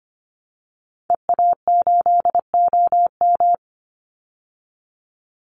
Morse code, heard as a keyed tone, about 25 wpm, 700 Hz. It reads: EA8OM
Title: Call Signs Repeated 2x - 25wpm